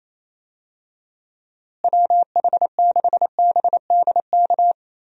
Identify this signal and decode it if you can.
Morse code, heard as a keyed tone, about 28 wpm, 700 Hz. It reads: WH6BDK
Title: USA Call Signs - 6 Characters - Repeated 2x - 28wpm